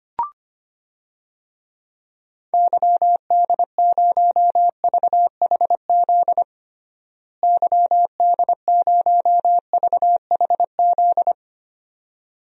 Morse code, heard as a keyed tone, about 25 wpm, 700 Hz. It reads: YD0VHZ YD0VHZ
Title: DXCC-100 Call Signs - 6 Characters - Repeated 2x - 25wpm